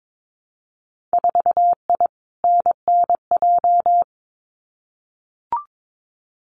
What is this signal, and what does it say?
Morse code, heard as a keyed tone, about 22 wpm, 700 Hz. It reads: 4I NNJ
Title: Winter Field Day Exchanges 22wpm